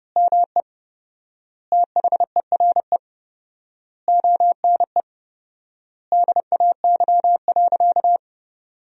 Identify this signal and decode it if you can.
Morse code, heard as a keyed tone, about 31 wpm, 700 Hz. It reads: ME THERE ONE DAY.
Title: 4x Word-Spacing - Sentences from Top 300 Words - 30wpm